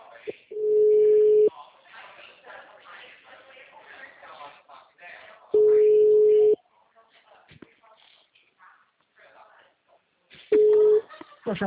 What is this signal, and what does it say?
Ringback tone (Europe, 425 Hz), with speech.
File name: out- -2013- - - .3675